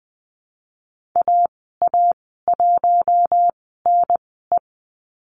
Morse code, heard as a keyed tone, about 20 wpm, spaced as for 15 wpm, 700 Hz. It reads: AA1NE
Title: 055 Lesson 10g - Letter r - Call Signs 20wpm